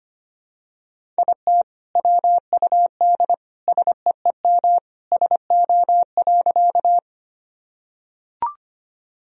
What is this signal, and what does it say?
Morse code, heard as a keyed tone, about 25 wpm, 700 Hz. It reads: IT WUD SEEM SO.
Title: Sentences from Top 300 Words and Abbreviations 25wpm